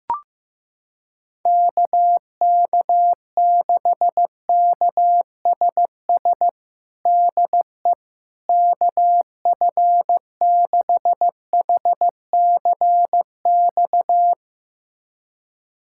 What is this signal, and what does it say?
Morse code, heard as a keyed tone, about 15 wpm, 700 Hz. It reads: KK6KSS DE KF6HCX
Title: USA Callsign DE Callsign - 6 Characters - 15wpm